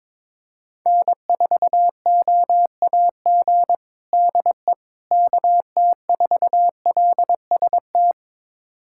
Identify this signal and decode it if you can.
Morse code, heard as a keyed tone, about 22 wpm, 700 Hz. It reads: N4OAG DE KT4LST
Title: USA Callsign DE Callsign 22wpm